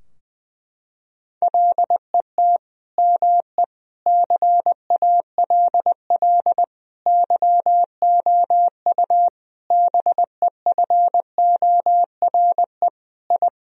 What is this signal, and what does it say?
Morse code, heard as a keyed tone, about 21 wpm, 700 Hz. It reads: LET ME CALL YOU BEFORE I
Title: Sentences from Top 1000 Words - Review Part 2 - 20wpm